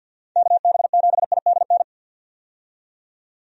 Morse code, heard as a keyed tone, about 51 wpm, 700 Hz. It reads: KB7IDN
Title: USA Call Signs - 6 Characters - 50wpm